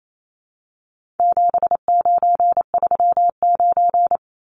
Morse code, heard as a keyed tone, about 28 wpm, 700 Hz. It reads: 7939
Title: Numbers - 4-Digit 28wpm